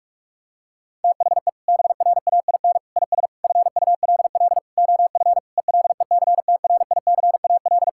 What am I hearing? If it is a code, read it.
Morse code, heard as a keyed tone, about 46 wpm, 700 Hz. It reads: THE BRAIN IS FULL OF ELECTRICAL
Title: Sentences from Top 1000 Words - Review Part 6 - 45wpm